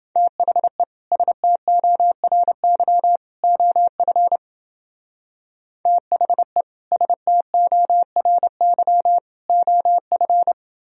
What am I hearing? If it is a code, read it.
Morse code, heard as a keyed tone, about 28 wpm, 700 Hz. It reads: THE STORY OF THE STORY OF